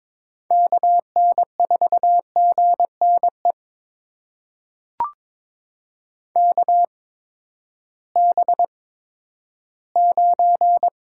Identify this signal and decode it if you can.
Morse code, heard as a keyed tone, about 22 wpm, spaced as for 12 wpm, 700 Hz. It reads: KN4GNE KB9
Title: USA Call Signs - 6 Characters - 8x Character Spacing - 22wpm